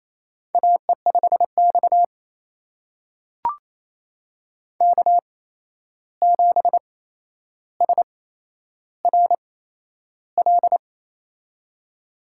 Morse code, heard as a keyed tone, about 28 wpm, spaced as for 11 wpm, 700 Hz. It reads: AE5X K7SRL
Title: USA Call Signs - 8x Character Spacing - 28wpm